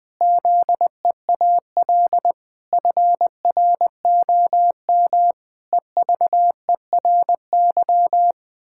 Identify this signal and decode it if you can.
Morse code, heard as a keyed tone, about 20 wpm, 700 Hz. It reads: ZEAL FROM EVERY
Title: Pangram Challenge v2 - 20wpm